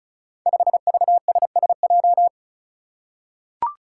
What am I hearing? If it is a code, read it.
Morse code, heard as a keyed tone, about 36 wpm, 700 Hz. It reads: 5VSSJ
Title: Five Challenging Letter-Number Combinations 35wpm